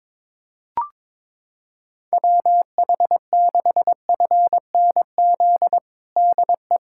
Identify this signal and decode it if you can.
Morse code, heard as a keyed tone, about 22 wpm, 700 Hz. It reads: WH6FNZ DE
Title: USA Callsign DE Callsign - 6 Characters - 22wpm